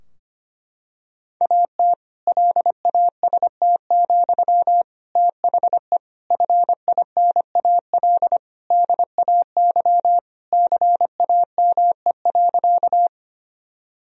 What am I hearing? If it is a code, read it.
Morse code, heard as a keyed tone, about 25 wpm, 700 Hz. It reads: AT LAST, THE FINAL DAY CAME.
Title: Sentences from Top 1000 Words - Review Part 2 - 25wpm